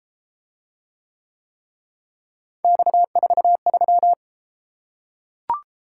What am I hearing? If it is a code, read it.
Morse code, heard as a keyed tone, about 29 wpm, 700 Hz. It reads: X43